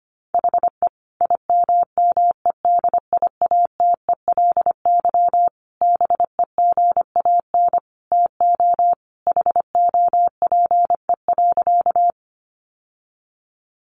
Morse code, heard as a keyed tone, about 25 wpm, 700 Hz. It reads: HE IMMEDIATELY BEGAN TO HOPE.
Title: Sentences from Top 1100 Words 25wpm